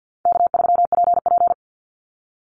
Morse code, heard as a keyed tone, about 49 wpm, 700 Hz. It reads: K3PP
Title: Easy Call Signs Repeated 2x - 50wpm